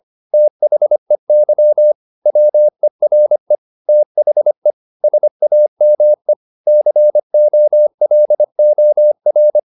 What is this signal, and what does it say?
Morse code, heard as a keyed tone, about 25 wpm, 600 Hz. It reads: THEY WERE THE SAME COLOR
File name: the old man and the sea part 1 25wpm